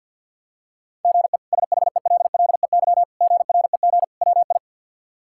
Morse code, heard as a keyed tone, about 46 wpm, 700 Hz. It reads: GE SHELLEY GREG WI
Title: K1USN SST QSO Practice - 50wpm